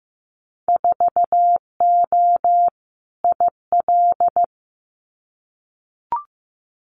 Morse code, heard as a keyed tone, about 15 wpm, 700 Hz. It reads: 4O IL
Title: Winter Field Day Exchanges Encore 15wpm